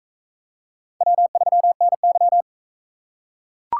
Morse code, heard as a keyed tone, about 40 wpm, 700 Hz. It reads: W3NY